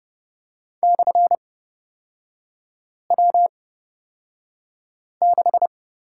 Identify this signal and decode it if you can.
Morse code, heard as a keyed tone, about 30 wpm, spaced as for 5 wpm, 700 Hz. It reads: /W6
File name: Lesson 36e - BT - Three Characters 30wpm